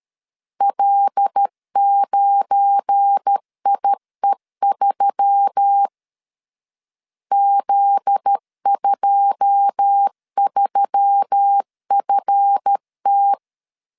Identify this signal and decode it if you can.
Morse code, heard as a keyed tone, about 13 wpm, 800 Hz. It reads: L9IE3 Z23FT
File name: LEZIONI / 9 2 12wpm 2301